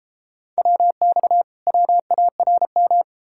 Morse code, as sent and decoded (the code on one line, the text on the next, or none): .-- -..- / .-- .- .-. --
WX WARM